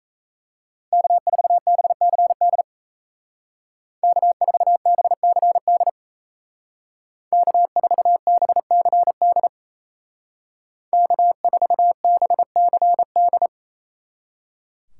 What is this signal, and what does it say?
Morse code, heard as a keyed tone, about 33 wpm, 700 Hz. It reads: K4BCD K4BCD K4BCD K4BCD
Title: Speed Racing - USA Call Signs 28wpm